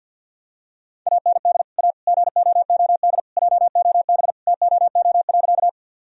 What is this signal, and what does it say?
Morse code, heard as a keyed tone, about 50 wpm, 700 Hz. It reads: AND A GOOD JOB TOO.
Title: Sentences from Top 1000 Words - Review Part 3 - 50wpm